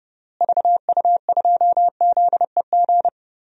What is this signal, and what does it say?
Morse code, heard as a keyed tone, about 30 wpm, 700 Hz. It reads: VU2ZEG
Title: DXCC-100 Call Signs - 6 Characters - Repeated 3x - 30wpm